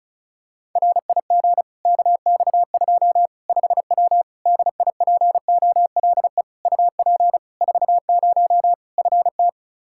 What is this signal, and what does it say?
Morse code, heard as a keyed tone, about 34 wpm, 700 Hz. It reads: RIG KX2 5W DIPOLE UP 40 FT
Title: Practice QSO - Exchange 2 - Info - 35wpm